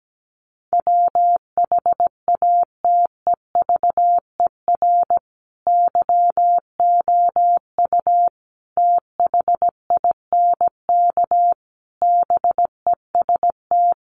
Morse code, heard as a keyed tone, about 17 wpm, 700 Hz. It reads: WHATEVER YOU THINK BEST
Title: Sentences from Top 1100 Words 17wpm